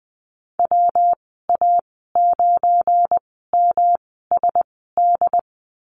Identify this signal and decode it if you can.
Morse code, heard as a keyed tone, about 20 wpm, spaced as for 14 wpm, 700 Hz. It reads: WA9MSD
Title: 193 Lesson 31g - Letter v - Call Signs 20wpm